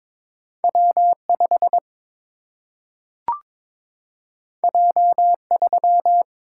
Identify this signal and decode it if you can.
Morse code, heard as a keyed tone, about 22 wpm, 700 Hz. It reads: W 5 J 3
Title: Single-Letter Single-Number 22wpm